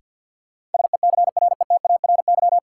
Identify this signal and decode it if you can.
Morse code, heard as a keyed tone, about 50 wpm, 700 Hz. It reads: SECRETARY
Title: Sentences from Top 1300 Words - New Word Drill - 50wpm